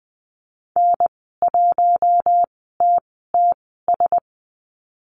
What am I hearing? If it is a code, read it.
Morse code, heard as a keyed tone, about 20 wpm, spaced as for 14 wpm, 700 Hz. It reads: N1TTS
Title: 062 Lesson 11g - Letter h - Call Signs 20wpm